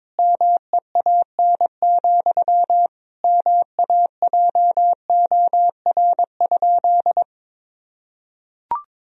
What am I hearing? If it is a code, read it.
Morse code, heard as a keyed tone, about 21 wpm, 700 Hz. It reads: MEAN, MAJOR?
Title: Sentences from Top 1000 Words - Review Part 4 - 22wpm